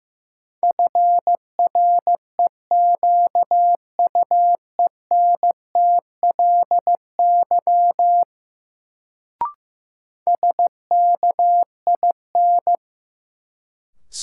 Morse code, heard as a keyed tone, about 15 wpm, 700 Hz. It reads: FREQUENTLY SKIN
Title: Sentences from Top 1300 Words - New Word Drill - 15wpm